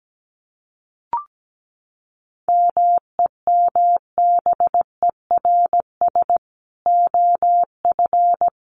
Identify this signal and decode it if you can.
Morse code, heard as a keyed tone, about 17 wpm, 700 Hz. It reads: MEMBERS OF
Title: Sets of 3 Words 17wpm